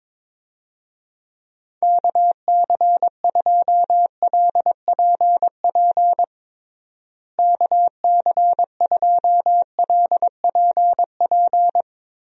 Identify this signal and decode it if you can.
Morse code, heard as a keyed tone, about 22 wpm, 700 Hz. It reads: KC2LPP KC2LPP
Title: USA Call Signs - 6 Characters - Repeated 3x - 22wpm